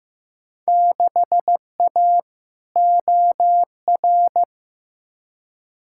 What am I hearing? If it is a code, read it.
Morse code, heard as a keyed tone, about 15 wpm, 700 Hz. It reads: 6A OR
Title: Field Day Exchanges - 15wpm